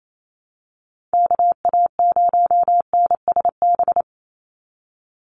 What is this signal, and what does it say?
Morse code, heard as a keyed tone, about 28 wpm, 700 Hz. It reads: KA0NSB
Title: USA Call Signs 28wpm